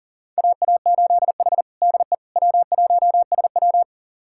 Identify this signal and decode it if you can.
Morse code, heard as a keyed tone, about 37 wpm, 700 Hz. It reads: AA8H DE W1SW
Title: USA Callsign DE Callsign - 4 Characters - 40wpm